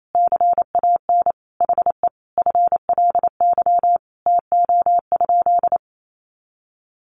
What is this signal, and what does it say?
Morse code, heard as a keyed tone, about 28 wpm, 700 Hz. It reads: CAN HE FLY TO?
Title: Sentences from Top 500 Words 28wpm